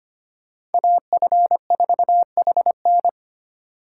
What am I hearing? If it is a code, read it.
Morse code, heard as a keyed tone, about 31 wpm, 700 Hz. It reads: AF4HN
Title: Speed Racing - USA 2x2 Call Signs 25wpm